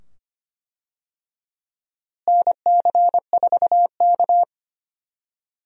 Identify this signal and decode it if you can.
Morse code, heard as a keyed tone, about 25 wpm, 700 Hz. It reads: NC4K
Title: USA Call Signs - 4 Characters - Repeated 2x - 25wpm